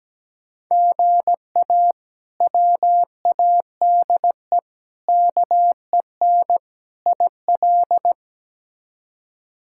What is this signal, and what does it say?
Morse code, heard as a keyed tone, about 17 wpm, 700 Hz. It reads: GA WADE KEN IL
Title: K1USN SST QSO Practice - 17wpm